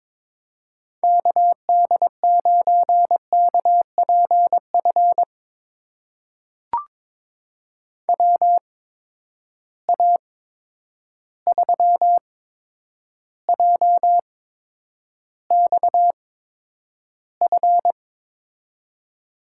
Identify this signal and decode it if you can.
Morse code, heard as a keyed tone, about 22 wpm, spaced as for 9 wpm, 700 Hz. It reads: KD9KPF WA3JXF
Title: USA Call Signs - 6 Characters - 8x Character Spacing - 22wpm